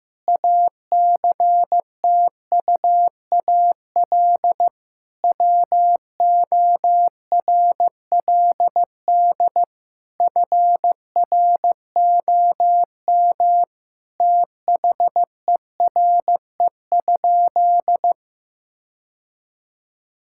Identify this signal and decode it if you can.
Morse code, heard as a keyed tone, about 15 wpm, 700 Hz. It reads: ACTUAL WORLD FROM THERE?
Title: Sentences from Top 1000 Words - Review Part 5 - 15wpm